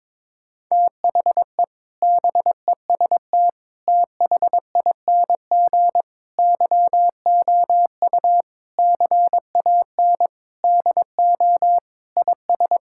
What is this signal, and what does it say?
Morse code, heard as a keyed tone, about 22 wpm, 700 Hz. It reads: THE BEST THING YOU CAN DO IS
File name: practice / 22wpm / Sets of 7 Words - 22wpm